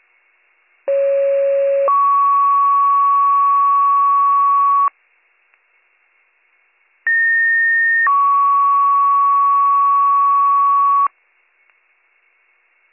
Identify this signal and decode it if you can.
Two-tone page, heard as a one-second tone, then a three-second tone, about 550 then 1120 Hz, about 1850 then 1120 Hz. It is heard from 0.9 to 11.1 s.